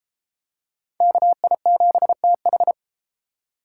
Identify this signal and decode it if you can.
Morse code, heard as a keyed tone, about 30 wpm, 700 Hz. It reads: KI7TH